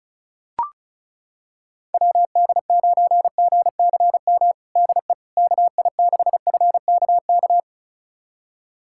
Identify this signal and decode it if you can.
Morse code, heard as a keyed tone, about 36 wpm, 700 Hz. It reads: WD9GCM DE KI6FKK
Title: USA Callsign DE Callsign - 6 Characters - 35wpm